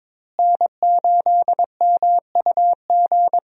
Morse code, heard as a keyed tone, about 22 wpm, 700 Hz. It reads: N8MUG
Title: USA Call Signs - 5 Characters - Repeated 3x - 22wpm